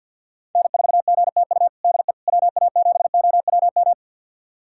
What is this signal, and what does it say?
Morse code, heard as a keyed tone, about 53 wpm, 700 Hz. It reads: N4GTU DE WA7OWM
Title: USA Callsign DE Callsign 50wpm